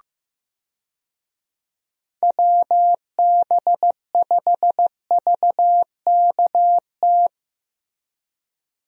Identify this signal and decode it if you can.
Morse code, heard as a keyed tone, about 15 wpm, 700 Hz. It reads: WB5VKT
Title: USA Call Signs - 6 Characters - Repeated 3x - 15wpm